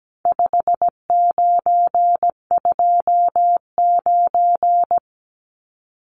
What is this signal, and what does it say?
Morse code, heard as a keyed tone, about 17 wpm, 700 Hz. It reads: 5929
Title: Numbers - 4-Digit 17wpm